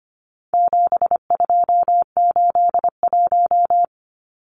Morse code, heard as a keyed tone, about 25 wpm, 700 Hz. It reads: 7281
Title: Numbers - 4-Digit 25wpm